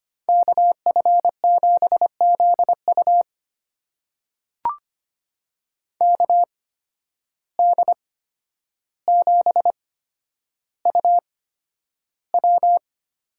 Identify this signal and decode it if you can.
Morse code, heard as a keyed tone, about 25 wpm, spaced as for 11 wpm, 700 Hz. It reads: KF7ZU KD7UW
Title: USA Call Signs - 5 Characters - 8x Character Spacing - 25wpm